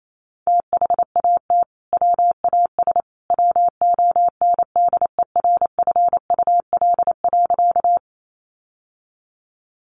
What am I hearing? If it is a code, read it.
Morse code, heard as a keyed tone, about 28 wpm, 700 Hz. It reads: THAT WAS WONDERFUL.
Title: Sentences from Top 1100 Words 28wpm